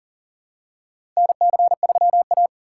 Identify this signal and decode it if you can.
Morse code, heard as a keyed tone, about 40 wpm, 700 Hz. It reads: NC3A
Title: USA Call Signs - 4 Characters - 4x Character Spacing - 40wpm